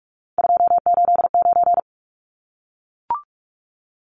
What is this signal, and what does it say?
Morse code, heard as a keyed tone, about 44 wpm, 700 Hz. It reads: 289
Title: Numbers - 3-Digit 45wpm